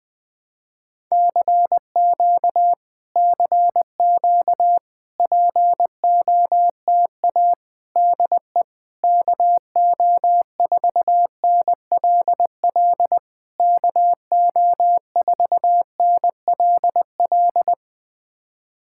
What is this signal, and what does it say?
Morse code, heard as a keyed tone, about 20 wpm, 700 Hz. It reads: CQ CQ POTA DE KO4NLL KO4NLL
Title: POTA QSO Practice - 20wpm